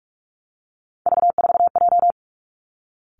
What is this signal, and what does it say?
Morse code, heard as a keyed tone, about 46 wpm, 700 Hz. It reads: V4J
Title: Three Challenging Letter-Number Combinations 45wpm